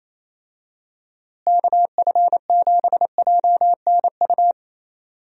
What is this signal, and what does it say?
Morse code, heard as a keyed tone, about 27 wpm, 700 Hz. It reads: KF7JNU
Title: USA Call Signs - 6 Characters - Repeated 2x - 28wpm